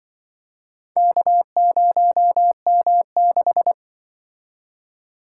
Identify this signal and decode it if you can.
Morse code, heard as a keyed tone, about 25 wpm, 700 Hz. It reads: K0M6
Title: Speed-Racing - Four Letter-Number Combinations - 20wpm